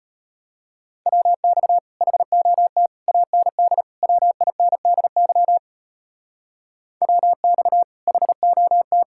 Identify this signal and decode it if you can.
Morse code, heard as a keyed tone, about 30 wpm, 700 Hz. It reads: WX HOT AND WINDY WX HOT
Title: Speed-Racing Common QSO Phrases Encore - 25wpm